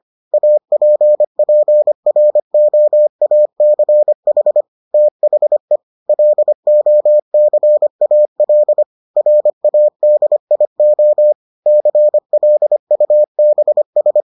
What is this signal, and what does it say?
Morse code, heard as a keyed tone, about 25 wpm, 600 Hz. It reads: APPROACH THE LOCAL RADIO CLUBS